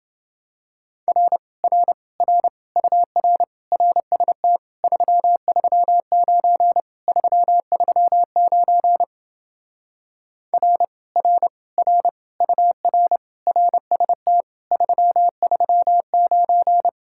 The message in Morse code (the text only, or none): R R R UR RST 339 339 R R R UR RST 339